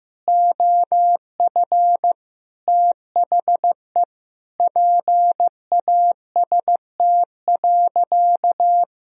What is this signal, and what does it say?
Morse code, heard as a keyed tone, about 15 wpm, 700 Hz. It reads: OF THE PAST.
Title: Sentences from Top 1000 Words - Review Part 3 - 15wpm